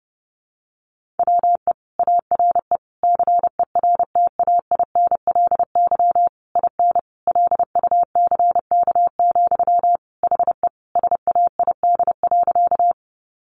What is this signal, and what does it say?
Morse code, heard as a keyed tone, about 30 wpm, 700 Hz. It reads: WE ARE CERTAINLY IN LUCK, HE SAID.
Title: Sentences from Top 1500 Words 30wpm